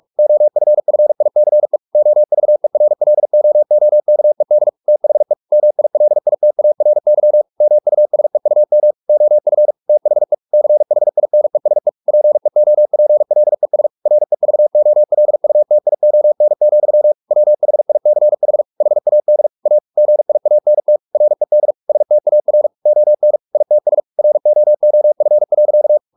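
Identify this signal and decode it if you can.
Morse code, heard as a keyed tone, about 44 wpm, 600 Hz. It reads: OFFICE OVERLOOKED THE MILITARY MUSEUM OF THE CHINESE PEOPLES REVOLUTION, WHICH HAD A GIANT RED STAR ON ITS ROOF.
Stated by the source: Morse practice at 45 wpm, the novel Never by Ken Follett in English, from CW med Gnister.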